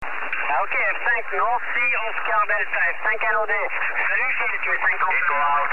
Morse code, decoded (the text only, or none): none